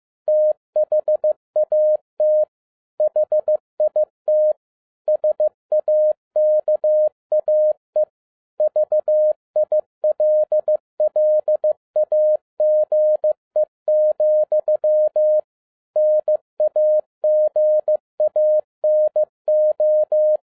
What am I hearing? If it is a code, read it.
Morse code, heard as a keyed tone, about 15 wpm, 610 Hz. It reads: THAT HIT SAKAE VILLAGE, NAGANO
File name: 2011-10-mp3 / n02-FIELD DAY CONTEST QRV FROM SAKAE VILLAGE 15wpm